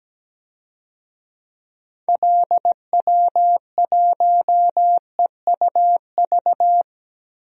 Morse code, heard as a keyed tone, about 17 wpm, 700 Hz. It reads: LW1EUV
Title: DXCC-100 Call Signs - 6 Characters - 17wpm